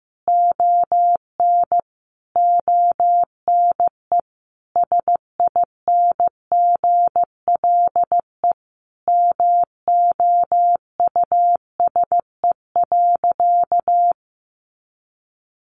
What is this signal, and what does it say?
Morse code, heard as a keyed tone, about 15 wpm, 700 Hz. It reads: ON ONE SINGLE MOUSE.
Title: A Sound Of Thunder 15wpm